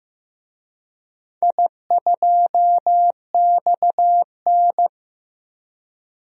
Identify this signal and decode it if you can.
Morse code, heard as a keyed tone, about 15 wpm, 700 Hz. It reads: I2XN
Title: DXCC-100 Call Signs - 4 Characters - 15wpm